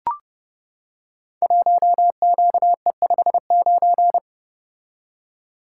Morse code, heard as a keyed tone, about 30 wpm, 700 Hz. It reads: 1QE59
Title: Code Groups - 5 Letter:Numbers - 30wpm